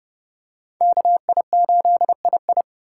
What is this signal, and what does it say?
Morse code, heard as a keyed tone, about 32 wpm, 700 Hz. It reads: KI8II
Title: USA Call Signs - 5 Characters - 2x Character Spacing - 30wpm